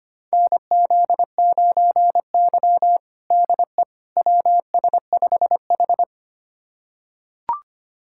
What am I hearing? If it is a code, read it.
Morse code, heard as a keyed tone, about 25 wpm, 700 Hz. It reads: NZ9Y DE WS5H
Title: USA Callsign DE Callsign - 4 Characters - 25wpm